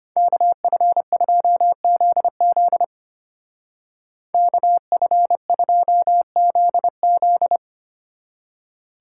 Morse code, heard as a keyed tone, about 31 wpm, 700 Hz. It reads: KF2ZZ KF2ZZ